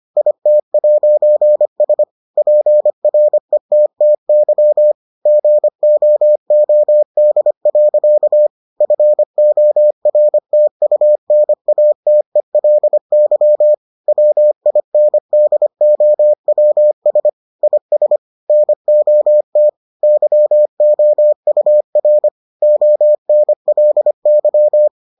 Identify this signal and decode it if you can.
Morse code, heard as a keyed tone, about 25 wpm, 600 Hz. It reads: IT'S PRETTY GOOD. FORTUNATELY WINDOWS IS NOT YOUR ONLY